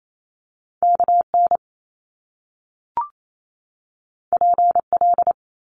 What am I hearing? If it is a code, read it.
Morse code, heard as a keyed tone, about 28 wpm, 700 Hz. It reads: KN PL
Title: Top Two Letter Combinations in Words 28wpm